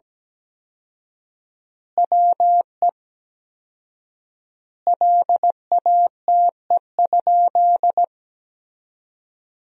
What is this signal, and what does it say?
Morse code, heard as a keyed tone, about 17 wpm, 700 Hz. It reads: WE LATE?
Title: 4x Word-Spacing - Sentences from Top 300 Words - 17wpm